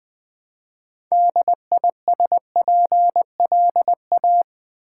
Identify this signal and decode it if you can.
Morse code, heard as a keyed tone, about 21 wpm, 700 Hz. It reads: DISPLA
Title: Build A Word - 2 Letters At A Time - 8-Letter Words - 20wpm